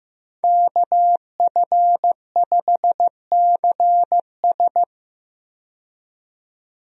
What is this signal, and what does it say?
Morse code, heard as a keyed tone, about 15 wpm, 700 Hz. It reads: KF5CS
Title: USA Call Signs - 5 Characters - 15wpm